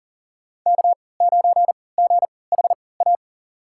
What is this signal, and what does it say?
Morse code, heard as a keyed tone, about 40 wpm, spaced as for 23 wpm, 700 Hz. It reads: K9GHA
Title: USA Call Signs - 3x Character Spacing - 40wpm